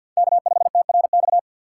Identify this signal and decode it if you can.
Morse code, heard as a keyed tone, about 47 wpm, 700 Hz. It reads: K5TRX
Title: USA Call Signs - 5 Characters - 3x Character Spacing - 50wpm